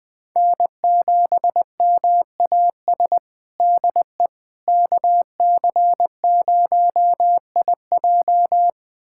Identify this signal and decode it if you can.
Morse code, heard as a keyed tone, about 21 wpm, 700 Hz. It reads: N7MAS DE KC0IJ
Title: USA Callsign DE Callsign - 5 Characters - 20wpm